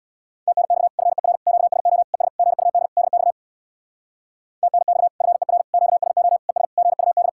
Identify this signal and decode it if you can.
Morse code, heard as a keyed tone, about 66 wpm, 700 Hz. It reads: TNX FER QSO ES GUD DX TNX FER QSO ES GUD